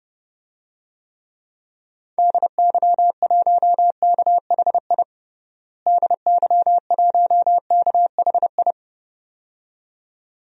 Morse code, heard as a keyed tone, about 32 wpm, 700 Hz. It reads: DY1KHI DY1KHI
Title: DXCC-100 Call Signs - 6 Characters - Repeated 2x - 30wpm